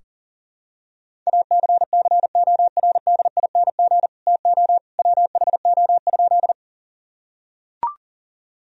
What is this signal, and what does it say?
Morse code, heard as a keyed tone, about 37 wpm, 700 Hz. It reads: ACCORDING TO WHO?
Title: Sentences from Top 1000 Words - Review Part 4 - 40wpm